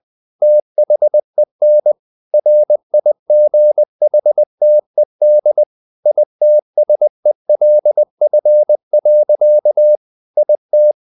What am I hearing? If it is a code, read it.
Morse code, heard as a keyed tone, about 20 wpm, 600 Hz. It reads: THEN RIGHTED ITSELF. IT
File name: the old man and the sea / the old man and the sea part 1 20wpm